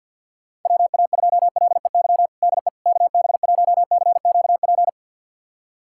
Morse code, heard as a keyed tone, about 45 wpm, 700 Hz. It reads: WA2LEY DE KB1CQP
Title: USA Callsign DE Callsign 50wpm